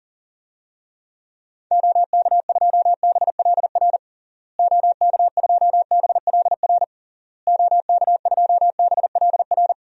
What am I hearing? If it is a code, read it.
Morse code, heard as a keyed tone, about 40 wpm, 700 Hz. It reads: OK2BLR OK2BLR OK2BLR
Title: DXCC-100 Call Signs - 6 Characters - Repeated 3x - 40wpm